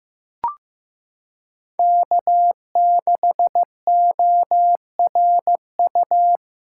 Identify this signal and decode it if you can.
Morse code, heard as a keyed tone, about 15 wpm, 700 Hz. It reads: K6ORU
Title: USA Callsign DE Callsign - 5 Characters - 15wpm